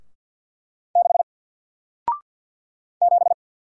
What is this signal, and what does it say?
Morse code, heard as a keyed tone, about 50 wpm, 700 Hz. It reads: 6 7
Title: Single Letter-Number Alt 50wpm